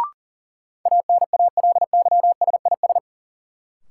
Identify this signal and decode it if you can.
Morse code, heard as a keyed tone, about 40 wpm, 700 Hz. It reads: ANALYSIS
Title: Sentences from Top 1400 Words - New Word Drill Alt - 40wpm